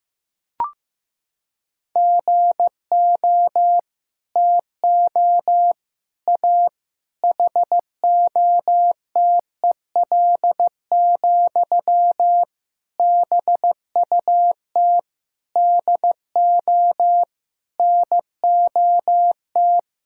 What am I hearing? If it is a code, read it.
Morse code, heard as a keyed tone, about 15 wpm, 700 Hz. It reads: GO TO A HOTEL, BUT DO NOT
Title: Sentences from Top 1600 Words 15wpm